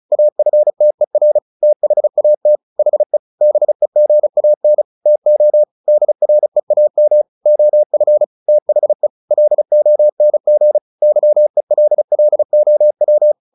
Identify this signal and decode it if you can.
Morse code, heard as a keyed tone, about 34 wpm, 600 Hz. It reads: AFTER THAT HE BEGAN TO DREAM OF THE LONG YELLOW
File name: the old man and the sea 2 35wpm